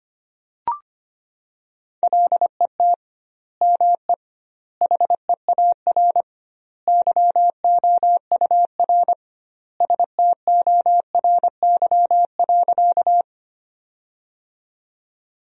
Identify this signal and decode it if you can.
Morse code, heard as a keyed tone, about 25 wpm, 700 Hz. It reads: LET ME HEAR YOUR STORY.
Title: 2x Word-Spacing - Sentences from Top 300 Words - 25wpm